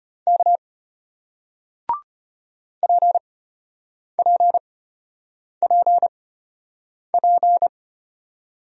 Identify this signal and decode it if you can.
Morse code, heard as a keyed tone, about 34 wpm, spaced as for 8 wpm, 700 Hz. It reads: K PPPP